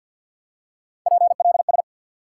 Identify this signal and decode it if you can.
Morse code, heard as a keyed tone, about 52 wpm, 700 Hz. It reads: PLS
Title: Top 90 QSO Elements 50wpm